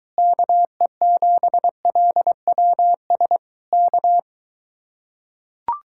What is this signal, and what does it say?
Morse code, heard as a keyed tone, about 23 wpm, 700 Hz. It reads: KE7LWS K